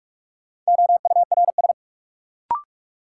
Morse code, heard as a keyed tone, about 43 wpm, 700 Hz. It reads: OURS
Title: Sentences from Top 1300 Words - New Word Drill Alt - 45wpm